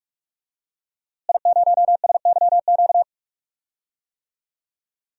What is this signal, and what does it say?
Morse code, heard as a keyed tone, about 45 wpm, 700 Hz. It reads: I0SYQ
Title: DXCC-100 Call Signs - 5 Characters - Repeated 3x - 45wpm